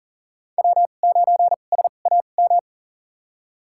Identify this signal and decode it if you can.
Morse code, heard as a keyed tone, about 40 wpm, spaced as for 29 wpm, 700 Hz. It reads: W9SAM
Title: USA Call Signs - 5 Characters - 2x Character Spacing - 40wpm